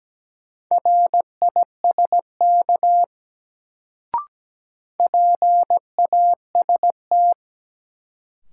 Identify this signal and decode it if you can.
Morse code, heard as a keyed tone, about 17 wpm, 700 Hz. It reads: RISK PAST
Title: Sentences from Top 600 Words - New Word Drill - 17wpm